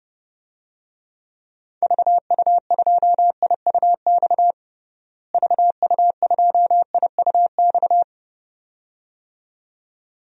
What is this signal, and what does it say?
Morse code, heard as a keyed tone, about 30 wpm, 700 Hz. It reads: VU2IUX VU2IUX
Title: DXCC-100 Call Signs - 6 Characters - Repeated 2x - 30wpm